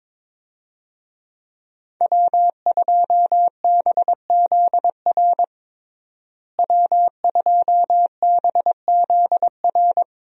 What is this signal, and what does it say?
Morse code, heard as a keyed tone, about 22 wpm, 700 Hz. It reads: W2BZR W2BZR
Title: USA Call Signs - 5 Characters - Repeated 3x - 22wpm